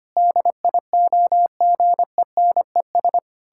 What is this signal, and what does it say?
Morse code, heard as a keyed tone, about 25 wpm, 700 Hz. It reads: DIOGENES
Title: Pangram Challenge v2 - 25wpm